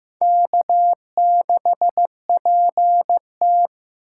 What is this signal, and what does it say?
Morse code, heard as a keyed tone, about 15 wpm, 700 Hz. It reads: K6PT
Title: USA Call Signs - 4 Characters - 8x Character Spacing - 15wpm